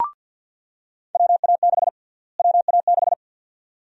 Morse code, heard as a keyed tone, about 53 wpm, 700 Hz. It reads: WA6 WA6
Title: USA Call Sign Prefixes - Repeated 2x - 50wpm